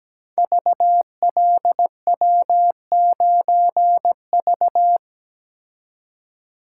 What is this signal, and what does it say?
Morse code, heard as a keyed tone, about 17 wpm, 700 Hz. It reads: VLW9V
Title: Code Groups - 5 Letter:Numbers - 17wpm